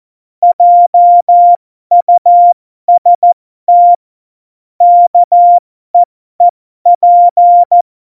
Morse code, heard as a keyed tone, about 14 wpm, spaced as for 12 wpm, 700 Hz. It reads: JUST KEEP
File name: Morsekurs / Witze